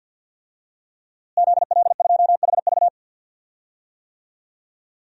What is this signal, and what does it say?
Morse code, heard as a keyed tone, about 47 wpm, 700 Hz. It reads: ZL2HV
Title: DXCC-100 Call Signs - 5 Characters - Repeated 3x - 50wpm